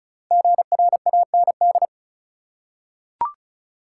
Morse code, heard as a keyed tone, about 34 wpm, 700 Hz. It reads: GRAND
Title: Five Letter Words - 35wpm